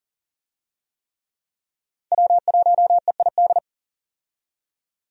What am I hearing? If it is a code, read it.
Morse code, heard as a keyed tone, about 40 wpm, 700 Hz. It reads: W1EID
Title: USA Call Signs - 5 Characters - 40wpm